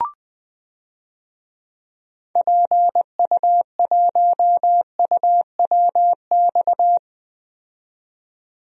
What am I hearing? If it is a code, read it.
Morse code, heard as a keyed tone, about 21 wpm, 700 Hz. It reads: PU1UWX
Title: DXCC-100 Call Signs - 6 Characters - 20wpm